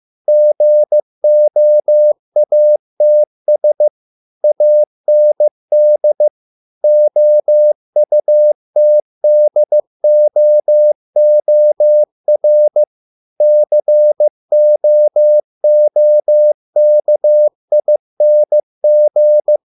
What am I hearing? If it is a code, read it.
Morse code, heard as a keyed tone, about 15 wpm, 600 Hz. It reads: GOATS AND OUTDOOR COOKING